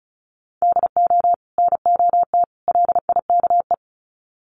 Morse code, heard as a keyed tone, about 34 wpm, 700 Hz. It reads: DO NOT LIKE
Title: Sets of 3 Words 35wpm